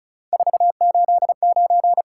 Morse code, heard as a keyed tone, about 36 wpm, 700 Hz. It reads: 489
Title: Top 180 QSO Elements 35wpm